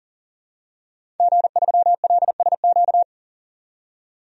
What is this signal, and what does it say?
Morse code, heard as a keyed tone, about 40 wpm, 700 Hz. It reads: G3LSQ
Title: DXCC-100 Call Signs - 5 Characters - 40wpm